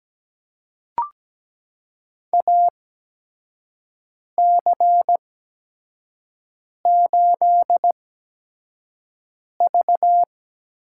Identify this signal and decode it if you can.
Morse code, heard as a keyed tone, about 17 wpm, spaced as for 5 wpm, 700 Hz. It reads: AC8V
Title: USA Call Signs - 4 Characters - 8x Character Spacing - 17wpm